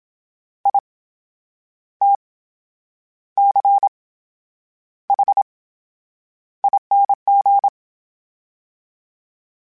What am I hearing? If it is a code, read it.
Morse code, heard as a keyed tone, about 27 wpm, spaced as for 9 wpm, 800 Hz. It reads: ITCHING
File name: Special INGs / Special INGs-mp3s / VerySlowToFaster Ing @25